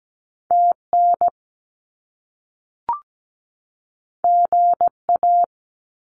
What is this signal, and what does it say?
Morse code, heard as a keyed tone, about 17 wpm, 700 Hz. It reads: TN GA
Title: US States 17wpm